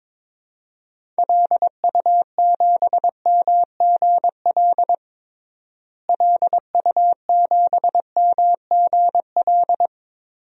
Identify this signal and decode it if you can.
Morse code, heard as a keyed tone, about 22 wpm, 700 Hz. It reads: LU7MGL LU7MGL
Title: DXCC-100 Call Signs - 6 Characters - Repeated 2x - 22wpm